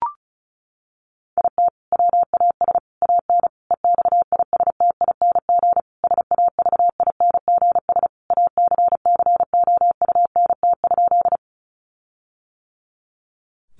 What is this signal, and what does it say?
Morse code, heard as a keyed tone, about 34 wpm, 700 Hz. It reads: IT WAS AN EXISTING SAVINGS ACCOUNT?
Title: Sentences from Top 1500 Words 35wpm